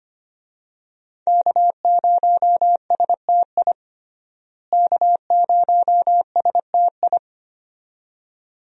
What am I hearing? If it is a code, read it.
Morse code, heard as a keyed tone, about 25 wpm, 700 Hz. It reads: K0STI K0STI
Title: USA Call Signs - 5 Characters - Repeated 2x - 25wpm